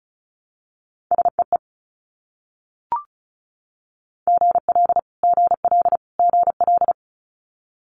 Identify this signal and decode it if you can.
Morse code, heard as a keyed tone, about 36 wpm, 700 Hz. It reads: SEE GL GL GL
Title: Multi-letter Phonemes Repeated 3x 35wpm